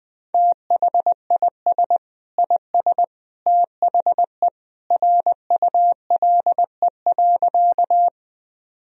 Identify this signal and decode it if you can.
Morse code, heard as a keyed tone, about 20 wpm, 700 Hz. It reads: THIS IS THE RULE.
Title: Sentences from Top 1000 Words - Review Part 2 - 20wpm